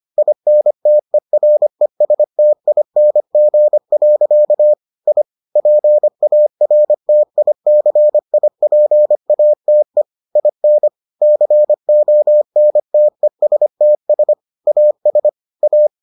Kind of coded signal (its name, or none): Morse code